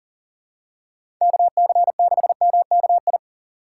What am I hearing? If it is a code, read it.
Morse code, heard as a keyed tone, about 39 wpm, 700 Hz. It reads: KC6MKI
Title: USA Call Signs - 6 Characters - 6x Character Spacing - 40wpm